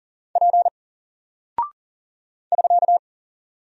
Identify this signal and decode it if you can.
Morse code, heard as a keyed tone, about 40 wpm, 700 Hz. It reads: P <SK>
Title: Single Character (Letter:Number:Punctuation:Prosign) Alt 40wpm